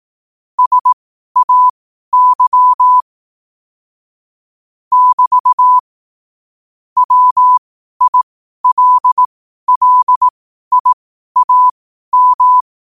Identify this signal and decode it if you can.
Morse code, heard as a keyed tone, about 18 wpm, spaced as for 12 wpm, 1000 Hz. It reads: SAY <BT> WILLIAM
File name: qotdfiles / 12wpm